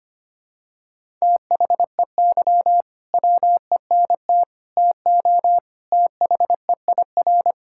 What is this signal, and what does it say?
Morse code, heard as a keyed tone, about 25 wpm, 700 Hz. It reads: THEY WENT TO THEIR
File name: practice / 25wpm / Sentences with 2-Word Phrases - 25wpm